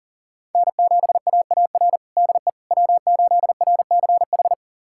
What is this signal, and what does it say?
Morse code, heard as a keyed tone, about 37 wpm, 700 Hz. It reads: N7AAR DE W8RCH
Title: USA Callsign DE Callsign - 5 Characters - 40wpm